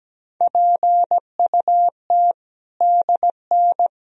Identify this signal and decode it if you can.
Morse code, heard as a keyed tone, about 17 wpm, 700 Hz. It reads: PUT DN
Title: Sentences from Top 400 Words and Abbreviations 17wpm